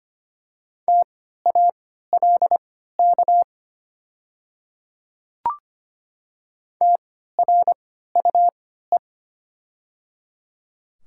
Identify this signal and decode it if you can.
Morse code, heard as a keyed tone, about 25 wpm, spaced as for 14 wpm, 690 Hz. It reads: TALK TRUE
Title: 3x Character Spacing - Top 500 Words - 25wpm